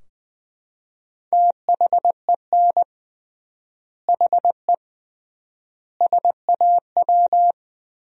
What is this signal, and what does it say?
Morse code, heard as a keyed tone, about 21 wpm, 700 Hz. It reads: THEN HE SAW
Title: 3x Word-Spacing - Sentences from Top 400 Words - 20wpm